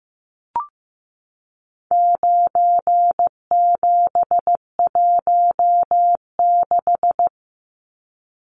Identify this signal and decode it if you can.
Morse code, heard as a keyed tone, about 15 wpm, 700 Hz. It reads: 9716
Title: Numbers - 4-Digit 15wpm